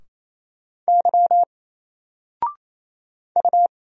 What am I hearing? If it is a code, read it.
Morse code, heard as a keyed tone, about 27 wpm, 700 Hz. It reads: Y U
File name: Single-Letter-Number-28wpm